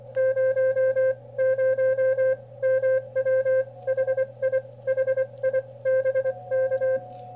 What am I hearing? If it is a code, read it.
Morse code, heard as a keyed tone, about 24 wpm, spaced as for 20 wpm, 520 Hz. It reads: 00MW HI HI BK
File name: qrp / wb0jnr at 400mW with ae5x 02-23-2018